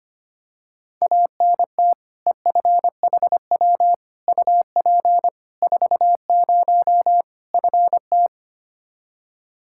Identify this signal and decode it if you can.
Morse code, heard as a keyed tone, about 25 wpm, 700 Hz. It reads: ANT EFHW UP 40 FT
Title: Practice QSO - Exchange 2 - Info - 25wpm